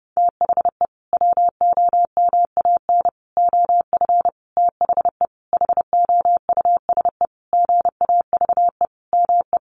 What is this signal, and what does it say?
Morse code, heard as a keyed tone, about 30 wpm, 700 Hz. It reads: THE WOMAN OF THE HOUSE GAVE ME
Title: Sentences from Top 1500 Words 30wpm